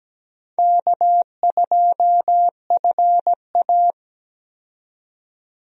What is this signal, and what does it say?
Morse code, heard as a keyed tone, about 17 wpm, 700 Hz. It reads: K2FA
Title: USA Call Signs - 4 Characters - 17wpm